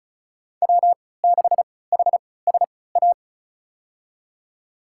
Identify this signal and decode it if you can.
Morse code, heard as a keyed tone, about 34 wpm, spaced as for 20 wpm, 700 Hz. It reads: W6HSA
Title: USA Call Signs - 5 Characters - 3x Character Spacing - 35wpm